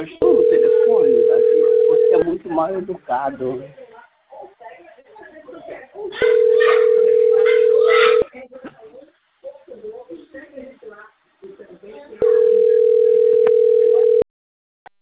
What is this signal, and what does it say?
Ringback tone (North America), 440 + 480 Hz, with speech.
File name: rg-1600-unknown- - - .69436